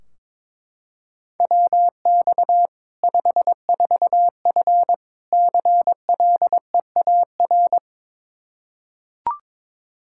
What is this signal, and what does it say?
Morse code, heard as a keyed tone, about 22 wpm, 700 Hz. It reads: WX 54F CLEAR
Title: K1USN SST QSO Practice 22wpm